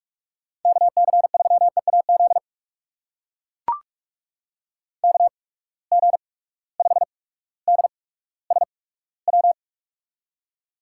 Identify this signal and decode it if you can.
Morse code, heard as a keyed tone, about 44 wpm, spaced as for 19 wpm, 700 Hz. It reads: KC3EAZ KG5DSW
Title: USA Call Signs - 6 Characters - 8x Character Spacing - 45wpm